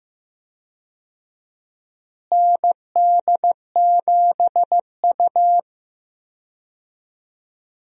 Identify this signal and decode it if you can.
Morse code, heard as a keyed tone, about 15 wpm, 700 Hz. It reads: ND7U
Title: USA Call Signs - 4 Characters - Repeated 3x - 15wpm